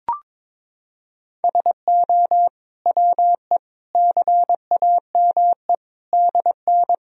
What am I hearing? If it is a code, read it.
Morse code, heard as a keyed tone, about 21 wpm, 700 Hz. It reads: SO WE CAME DN
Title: Sentences from Top 400 Words and Abbreviations 22wpm